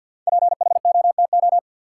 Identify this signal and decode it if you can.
Morse code, heard as a keyed tone, about 49 wpm, 700 Hz. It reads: PHOTO
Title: Sentences from Top 1200 Words - New Word Drill Alt - 50wpm